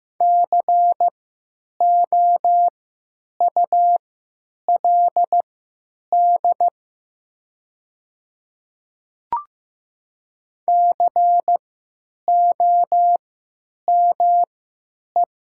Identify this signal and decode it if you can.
Morse code, heard as a keyed tone, about 15 wpm, spaced as for 9 wpm, 700 Hz. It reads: COULD COME
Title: 3x Character Spacing - Top 100 Words - 15wpm